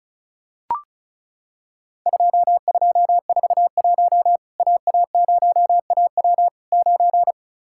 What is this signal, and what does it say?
Morse code, heard as a keyed tone, about 34 wpm, 700 Hz. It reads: 2241 AA0AW 9
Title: Worked All Europe DX Contest - QTC Exchange - 35wpm